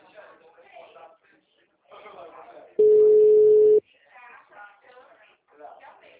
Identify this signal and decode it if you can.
Ringback tone (Europe, 425 Hz), with speech.